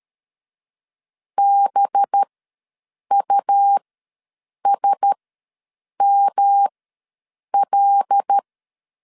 Morse code, heard as a keyed tone, about 13 wpm, spaced as for 7 wpm, 800 Hz. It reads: BUSML
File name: 5 LB ws 12wpm